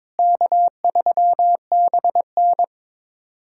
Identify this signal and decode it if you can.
Morse code, heard as a keyed tone, about 22 wpm, 700 Hz. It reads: K3BN
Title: USA Callsign DE Callsign - 4 Characters - 22wpm